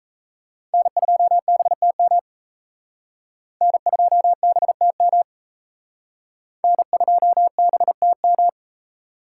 Morse code, heard as a keyed tone, about 34 wpm, 700 Hz. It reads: N2BTM N2BTM N2BTM
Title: Speed Racing - USA Call Signs 28wpm